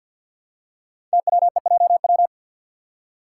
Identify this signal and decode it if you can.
Morse code, heard as a keyed tone, about 50 wpm, 690 Hz. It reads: TWEJW
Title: Code Groups - 5 Letters - 50wpm